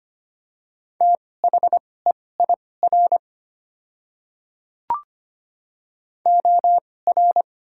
Morse code, heard as a keyed tone, about 25 wpm, spaced as for 18 wpm, 700 Hz. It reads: THEIR OR
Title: 2x Character Spacing - Top 200 Words - 25wpm